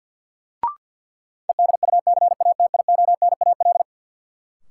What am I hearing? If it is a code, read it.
Morse code, heard as a keyed tone, about 47 wpm, 700 Hz. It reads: EDUCATIONAL
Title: Sentences from Top 1200 Words - New Word Drill Alt - 50wpm